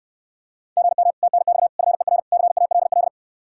Morse code, heard as a keyed tone, about 69 wpm, 700 Hz. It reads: GM TNX FER CALL